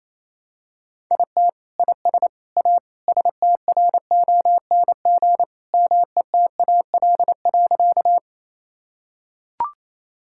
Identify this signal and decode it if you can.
Morse code, heard as a keyed tone, about 28 wpm, 700 Hz. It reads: IT IS A STRONG METAL.
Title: Sentences from Top 1000 Words - Review Part 6 - 28wpm